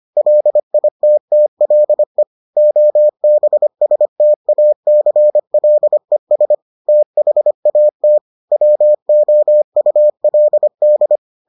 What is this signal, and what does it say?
Morse code, heard as a keyed tone, about 25 wpm, 600 Hz. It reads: LITTLE OBSTACLES THAT WOULD